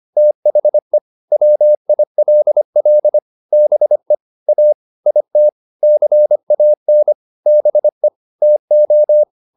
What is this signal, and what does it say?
Morse code, heard as a keyed tone, about 25 wpm, 600 Hz. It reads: THE WILL BE A IT CAN BE TO